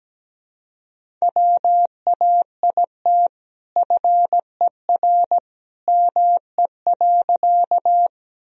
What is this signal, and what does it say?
Morse code, heard as a keyed tone, about 17 wpm, 700 Hz. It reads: WAIT FER ME.
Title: Sentences from Top 500 Words and Abbreviations 17wpm